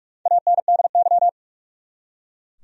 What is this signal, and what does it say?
Morse code, heard as a keyed tone, about 47 wpm, 700 Hz. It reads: ANDY
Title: USA Common Names - Max 5 Letters - 45wpm